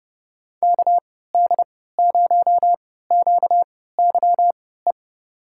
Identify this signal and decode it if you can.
Morse code, heard as a keyed tone, about 30 wpm, spaced as for 17 wpm, 700 Hz. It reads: KD0QYE
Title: USA Call Signs - 6 Characters - 3x Character Spacing - 30wpm